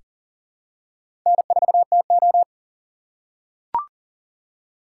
Morse code, heard as a keyed tone, about 40 wpm, 700 Hz. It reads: N4TO
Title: USA Call Signs - 4 Characters - 2x Character Spacing - 40wpm